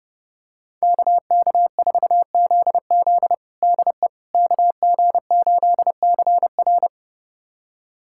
Morse code, heard as a keyed tone, about 30 wpm, 700 Hz. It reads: KK4ZZ DE KG8CR
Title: USA Callsign DE Callsign - 5 Characters - 30wpm